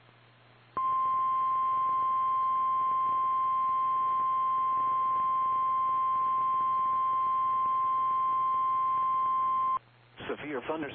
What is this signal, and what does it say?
Emergency Alert System (SAME) tones (the 1050 Hz attention tone), with speech. It is heard from 0.7 to 9.8 s.
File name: eas / KKPRKXPN / KKPRKXPN